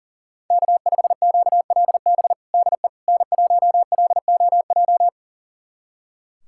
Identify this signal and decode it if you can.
Morse code, heard as a keyed tone, about 40 wpm, 700 Hz. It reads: K5QLB DE N1LOJ
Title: USA Callsign DE Callsign - 5 Characters - 40wpm